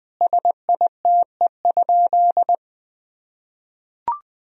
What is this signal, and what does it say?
Morse code, heard as a keyed tone, about 20 wpm, 700 Hz. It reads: SITE?
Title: Sentences from Top 1000 Words - Review Part 4 - 20wpm